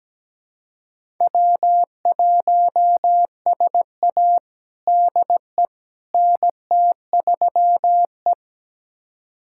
Morse code, heard as a keyed tone, about 17 wpm, 700 Hz. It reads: W1SA DE NT3E
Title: USA Callsign DE Callsign - 4 Characters - 17wpm